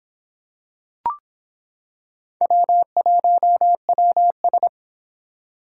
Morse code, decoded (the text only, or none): W1WS